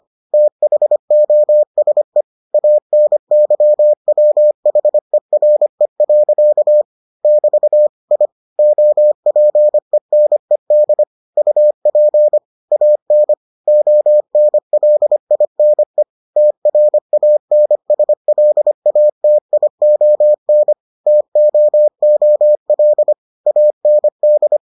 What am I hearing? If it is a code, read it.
Morse code, heard as a keyed tone, about 25 wpm, 600 Hz. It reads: THOSE ANYWHERE. <BT> I OPENED UP AN ONLINE TRANSLATION TOOL AND